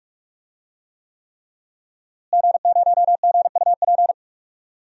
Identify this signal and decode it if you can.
Morse code, heard as a keyed tone, about 46 wpm, 700 Hz. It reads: G0GUP
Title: DXCC-100 Call Signs - 5 Characters - Repeated 3x - 45wpm